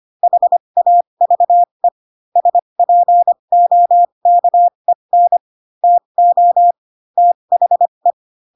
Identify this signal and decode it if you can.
Morse code, heard as a keyed tone, about 25 wpm, spaced as for 22 wpm, 700 Hz. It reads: HAVE SPOKEN TO THE
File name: Pinocchio / Pinocchio ch3 18 25 ik1hge